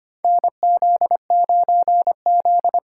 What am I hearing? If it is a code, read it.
Morse code, heard as a keyed tone, about 25 wpm, 700 Hz. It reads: NZ9Z
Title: USA Call Signs - 4 Characters - 2x Character Spacing - 25wpm